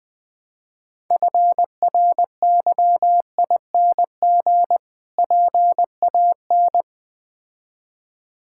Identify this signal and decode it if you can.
Morse code, heard as a keyed tone, about 20 wpm, 700 Hz. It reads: FRYING PAN
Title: Difficult Words with Hints - 20wpm